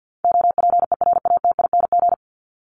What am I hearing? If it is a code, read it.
Morse code, heard as a keyed tone, about 48 wpm, 700 Hz. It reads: OPERATING
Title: ING Suffix - Top 100 Words with Base Words 50wpm